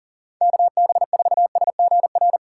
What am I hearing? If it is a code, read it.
Morse code, heard as a keyed tone, about 40 wpm, 700 Hz. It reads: KB4SGR
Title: USA Call Signs - 6 Characters - 3x Character Spacing - 40wpm